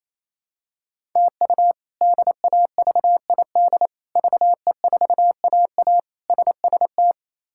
Morse code, heard as a keyed tone, about 28 wpm, 700 Hz. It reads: TU DAVID VE4AA SST
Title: K1USN SST QSO Practice - 28wpm